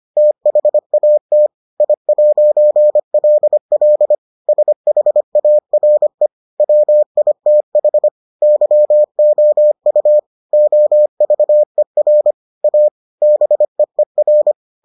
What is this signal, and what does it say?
Morse code, heard as a keyed tone, about 25 wpm, 600 Hz. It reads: THAT I'LL SHARE WITH YOU OVER A BEER